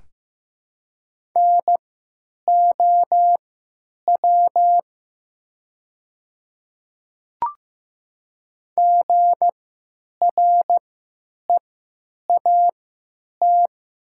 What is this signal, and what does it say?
Morse code, heard as a keyed tone, about 15 wpm, spaced as for 9 wpm, 700 Hz. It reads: NOW GREAT
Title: 3x Character Spacing - Top 500 Words - 15wpm